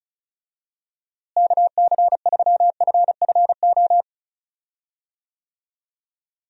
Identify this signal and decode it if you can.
Morse code, heard as a keyed tone, about 36 wpm, 700 Hz. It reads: KC3FFO
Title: USA Call Signs - 6 Characters - Repeated 3x - 35wpm